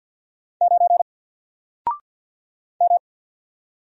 Morse code, heard as a keyed tone, about 48 wpm, 700 Hz. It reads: 9 M
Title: Single Letter-Number Alt 50wpm